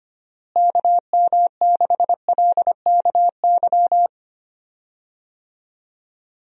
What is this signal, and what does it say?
Morse code, heard as a keyed tone, about 25 wpm, 700 Hz. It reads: KM6LKY
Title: USA Call Signs - 6 Characters - Repeated 2x - 25wpm